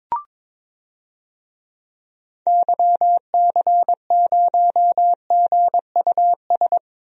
Morse code, heard as a keyed tone, about 21 wpm, 700 Hz. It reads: YC0GUS
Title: DXCC-100 Call Signs - 6 Characters - Repeated 3x - 22wpm